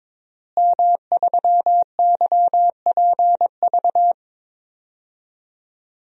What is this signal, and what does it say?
Morse code, heard as a keyed tone, about 22 wpm, 700 Hz. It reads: M3YPV
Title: DXCC-100 Call Signs - 5 Characters - Repeated 2x - 22wpm